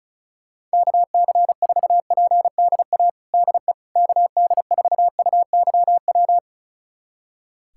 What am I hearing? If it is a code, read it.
Morse code, heard as a keyed tone, about 34 wpm, 700 Hz. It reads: KC4PDA DE KD4UYW
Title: USA Callsign DE Callsign - 6 Characters - 35wpm